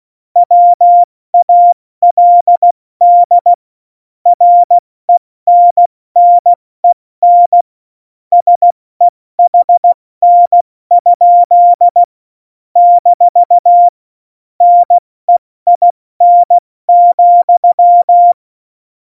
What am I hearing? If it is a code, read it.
Morse code, heard as a keyed tone, about 16 wpm, spaced as for 14 wpm, 700 Hz. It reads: WALD RENNEN SEHN? - NEIN,
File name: Morsekurs / Witze / Witz 03. Oktober 17, Tempo 70 F80